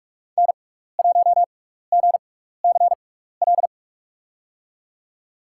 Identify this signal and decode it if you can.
Morse code, heard as a keyed tone, about 44 wpm, spaced as for 15 wpm, 700 Hz. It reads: N1GCL
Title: USA Call Signs - 5 Characters - 6x Character Spacing - 45wpm